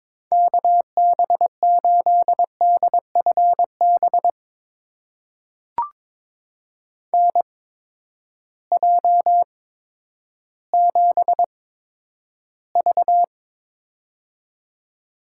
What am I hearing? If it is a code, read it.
Morse code, heard as a keyed tone, about 22 wpm, spaced as for 11 wpm, 700 Hz. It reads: KB8DFB NJ7V
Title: USA Call Signs - 8x Character Spacing - 22wpm